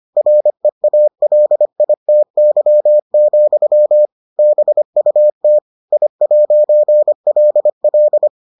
Morse code, heard as a keyed tone, about 25 wpm, 600 Hz. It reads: REALITY, BUT I'LL